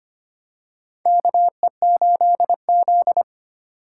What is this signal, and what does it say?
Morse code, heard as a keyed tone, about 25 wpm, 700 Hz. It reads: KE8Z
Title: USA Call Signs - 4 Characters - 4x Character Spacing - 25wpm